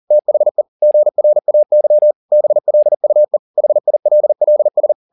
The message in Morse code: THE GRAY BLUE HILLS

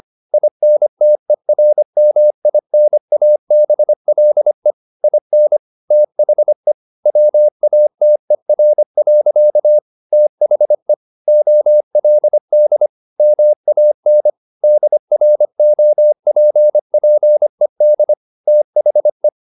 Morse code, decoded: INTERMINABLE IN THE WATER. THE OLD MAN DROPPED THE

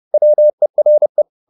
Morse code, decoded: WERE

